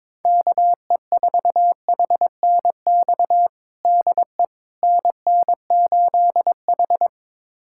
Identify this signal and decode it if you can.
Morse code, heard as a keyed tone, about 22 wpm, 700 Hz. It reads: KE4HNX DE NN8H
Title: USA Callsign DE Callsign 22wpm